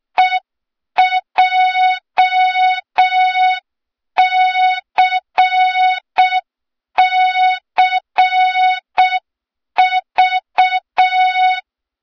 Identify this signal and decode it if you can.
Morse code, heard as a keyed tone, about 6 wpm, 3700 Hz. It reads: EJCCV
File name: Morze-HA2MN / hang / 23 GYAKORLAT